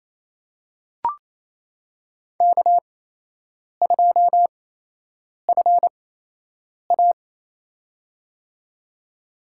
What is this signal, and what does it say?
Morse code, heard as a keyed tone, about 28 wpm, spaced as for 8 wpm, 700 Hz. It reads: K2FA
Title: USA Call Signs - 4 Characters - 8x Character Spacing - 28wpm